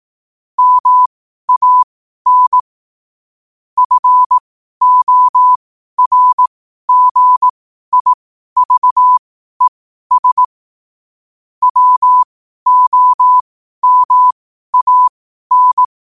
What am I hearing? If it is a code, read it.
Morse code, heard as a keyed tone, about 18 wpm, spaced as for 12 wpm, 1000 Hz. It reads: MAN FORGIVES WOMAN
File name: qotdfiles / 12wpm